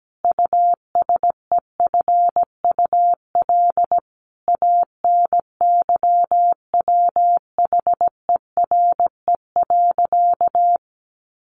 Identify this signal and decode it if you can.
Morse code, heard as a keyed tone, about 17 wpm, 700 Hz. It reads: USEFUL ANYWHERE.
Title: Sentences from Top 1500 Words 17wpm